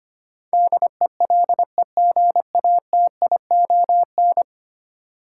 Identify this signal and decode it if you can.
Morse code, heard as a keyed tone, about 25 wpm, 700 Hz. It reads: DELEGATION
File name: Ten Letter Words - 25wpm